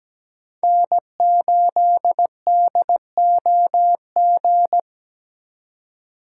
Morse code, heard as a keyed tone, about 20 wpm, 700 Hz. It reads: N8DOG
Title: Speed Racing - USA Call Signs 17wpm